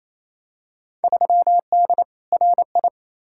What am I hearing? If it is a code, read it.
Morse code, heard as a keyed tone, about 28 wpm, 700 Hz. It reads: 3D RI
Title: Field Day Exchanges Encore - 28wpm